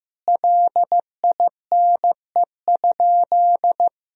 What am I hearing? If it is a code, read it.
Morse code, heard as a keyed tone, about 15 wpm, 700 Hz. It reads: LINE?
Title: Sentences from Top 1000 Words - Review Part 1 - 15wpm